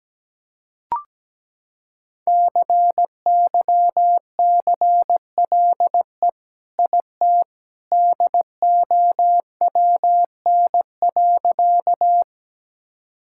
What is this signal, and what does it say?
Morse code, heard as a keyed tone, about 17 wpm, 700 Hz. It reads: CYCLE IT DOWN.
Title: Sentences from Top 1000 Words - Review Part 6 - 17wpm